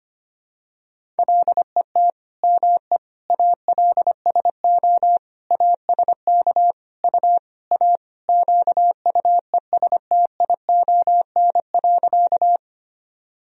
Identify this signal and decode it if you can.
Morse code, heard as a keyed tone, about 25 wpm, 700 Hz. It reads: LET ME ALSO ASK U A QUESTION.
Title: Sentences from Top 400 Words and Abbreviations 25wpm